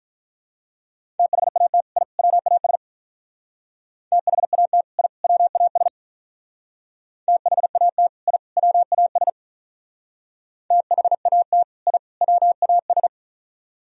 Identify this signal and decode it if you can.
Morse code, heard as a keyed tone, about 46 wpm, 700 Hz. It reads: THAT I WAS THAT I WAS THAT I WAS THAT I WAS